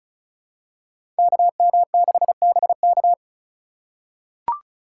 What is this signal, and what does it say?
Morse code, heard as a keyed tone, about 36 wpm, 700 Hz. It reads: KM6BK
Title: USA Call Signs - 3x Character Spacing - 35wpm